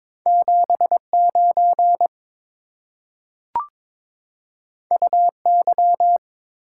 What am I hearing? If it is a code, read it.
Morse code, heard as a keyed tone, about 22 wpm, 690 Hz. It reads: 79 UY
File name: practice / 22wpm / Two Letter-Number Combinations - 22wpm